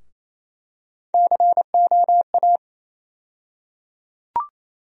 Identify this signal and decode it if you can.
Morse code, heard as a keyed tone, about 28 wpm, 700 Hz. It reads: COA
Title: USA Call Sign Postfixes - 3 Letters 28wpm